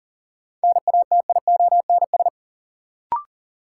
Morse code, heard as a keyed tone, about 40 wpm, 700 Hz. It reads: NATIONS
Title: Sets of 3 Words - New Word Drill - 40wpm